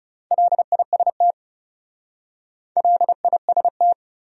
Morse code, heard as a keyed tone, about 37 wpm, 700 Hz. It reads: LIST LIST
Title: Speed-Racing Top 500 Words 30wpm